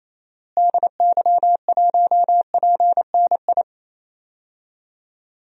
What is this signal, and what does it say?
Morse code, heard as a keyed tone, about 28 wpm, 700 Hz. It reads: DY1PNI
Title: DXCC-100 Call Signs - 6 Characters - Repeated 2x - 28wpm